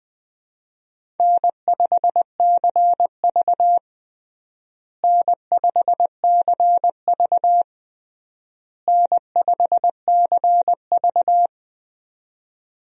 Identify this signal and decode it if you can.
Morse code, heard as a keyed tone, about 20 wpm, 700 Hz. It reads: N5CV N5CV N5CV
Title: USA Call Signs - 4 Characters - Repeated 3x - 20wpm